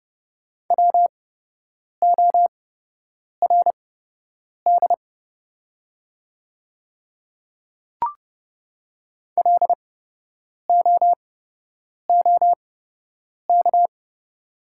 Morse code, heard as a keyed tone, about 30 wpm, spaced as for 8 wpm, 690 Hz. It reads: WORD LOOK
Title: 8x Character Spacing - Top 100 Words - 30wpm